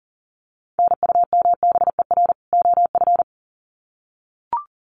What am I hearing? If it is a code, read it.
Morse code, heard as a keyed tone, about 37 wpm, 700 Hz. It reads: NUMBER OF
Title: Sets of 2 Words 40wpm